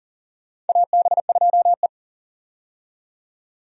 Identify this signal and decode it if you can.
Morse code, heard as a keyed tone, about 40 wpm, 700 Hz. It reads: AB2E
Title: USA Call Signs - 4 Characters - 40wpm